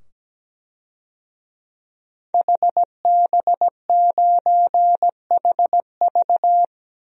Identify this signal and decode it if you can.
Morse code, heard as a keyed tone, about 17 wpm, 700 Hz. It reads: HB9HV
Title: DXCC-100 Call Signs - 5 Characters - 17wpm